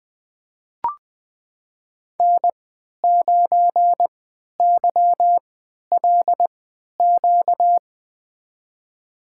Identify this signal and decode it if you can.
Morse code, heard as a keyed tone, about 20 wpm, spaced as for 11 wpm, 700 Hz. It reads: N9YLQ
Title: USA Call Signs - 3x Character Spacing - 20wpm